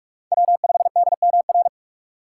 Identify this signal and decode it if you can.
Morse code, heard as a keyed tone, about 44 wpm, 690 Hz. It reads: W5DMR
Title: USA Call Signs - 4x Character Spacing - 45wpm